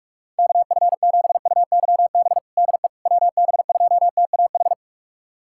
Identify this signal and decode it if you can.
Morse code, heard as a keyed tone, about 41 wpm, 700 Hz. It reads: KF7UYB DE WB2TAH
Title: USA Callsign DE Callsign - 6 Characters - 45wpm